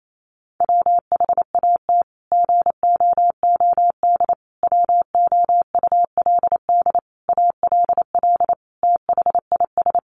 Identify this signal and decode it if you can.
Morse code, heard as a keyed tone, about 28 wpm, 700 Hz. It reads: WHAT GOOD WOULD ALL THIS
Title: Sentences from Top 1200 Words 28wpm